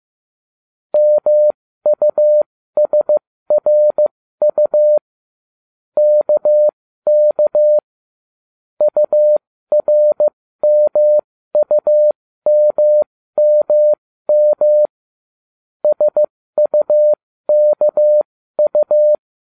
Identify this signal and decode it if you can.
Morse code, heard as a keyed tone, about 15 wpm, spaced as for 13 wpm, 600 Hz. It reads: MUSRU KK URMUMMM SUKU